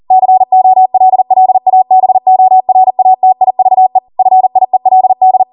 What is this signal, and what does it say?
Morse code, heard as a keyed tone, about 40 wpm, 750 Hz. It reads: COLLABORATIVE FIELD